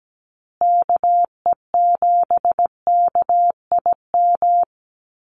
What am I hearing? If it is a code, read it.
Morse code, heard as a keyed tone, about 17 wpm, 700 Hz. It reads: KE7KIM
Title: USA Call Signs Repeated 3x 17wpm